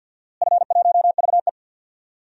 Morse code, heard as a keyed tone, about 52 wpm, 700 Hz. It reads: F1VE
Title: DXCC-100 Call Signs - 4 Characters - Repeated 2x - 50wpm